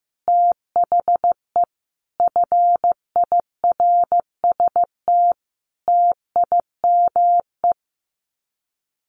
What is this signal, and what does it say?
Morse code, heard as a keyed tone, about 15 wpm, 700 Hz. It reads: THE FIRST TIME
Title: Sets of 5 Words Encore 15wpm